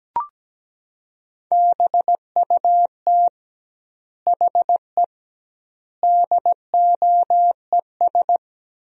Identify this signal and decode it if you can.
Morse code, heard as a keyed tone, about 17 wpm, 700 Hz. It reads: BUT HE DOES
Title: 2x Word-Spacing - Sentences from Top 200 Words - 17wpm